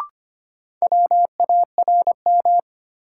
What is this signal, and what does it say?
Morse code, heard as a keyed tone, about 25 wpm, 700 Hz. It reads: WARM
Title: Sentences from Top 500 Words - New Word Drill - 25wpm